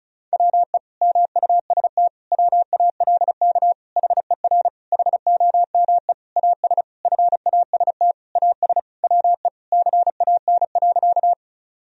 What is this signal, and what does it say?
Morse code, heard as a keyed tone, about 34 wpm, 700 Hz. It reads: WE MUST WALK HER HOME AS FAST AS WE CAN.
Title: Sentences from Top 1000 Words - Review Part 2 - 35wpm